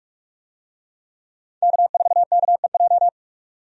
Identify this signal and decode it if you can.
Morse code, heard as a keyed tone, about 46 wpm, 700 Hz. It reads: K4KEJ
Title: USA Call Signs - 5 Characters - 45wpm